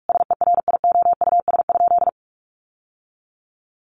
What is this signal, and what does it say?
Morse code, heard as a keyed tone, about 44 wpm, 700 Hz. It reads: SERIOUS?
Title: Sentences from Top 1100 Words 45wpm